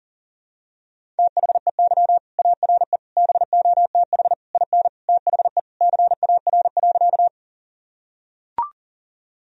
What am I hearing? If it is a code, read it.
Morse code, heard as a keyed tone, about 40 wpm, 700 Hz. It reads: THEY ARE BOTH IN THE CAR.
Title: Sentences from Top 1000 Words - Review Part 1 - 40wpm